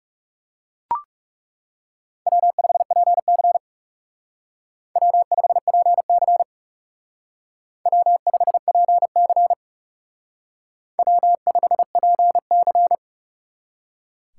Morse code, heard as a keyed tone, about 38 wpm, 700 Hz. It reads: W5PC W5PC W5PC W5PC